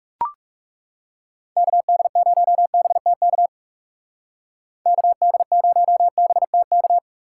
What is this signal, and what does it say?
Morse code, heard as a keyed tone, about 39 wpm, 700 Hz. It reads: KD0BTK KD0BTK